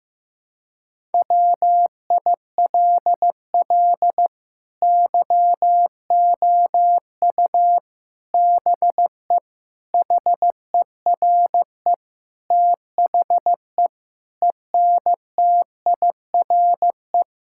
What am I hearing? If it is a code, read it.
Morse code, heard as a keyed tone, about 15 wpm, 700 Hz. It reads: WILL YOU BE HERE THE ENTIRE
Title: Sentences from Top 1000 Words - Review Part 5 - 15wpm